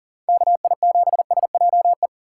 Morse code, heard as a keyed tone, about 40 wpm, 700 Hz. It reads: KI7SJE
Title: USA Call Signs - 6 Characters - 6x Character Spacing - 40wpm